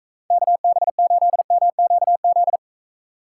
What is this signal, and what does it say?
Morse code, heard as a keyed tone, about 35 wpm, 700 Hz. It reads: KB8MQZ